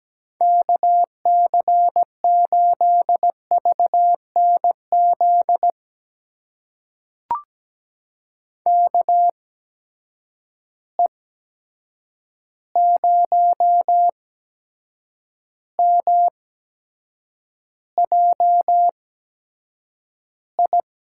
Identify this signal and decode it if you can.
Morse code, heard as a keyed tone, about 17 wpm, spaced as for 7 wpm, 700 Hz. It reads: KC8VNZ KE0MJI
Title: USA Call Signs - 6 Characters - 8x Character Spacing - 17wpm